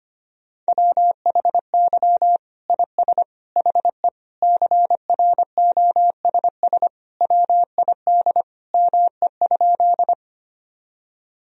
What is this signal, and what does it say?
Morse code, heard as a keyed tone, about 25 wpm, 700 Hz. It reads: WHY IS HE CROSS WID ME?
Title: Sentences from Top 300 Words and Abbreviations 25wpm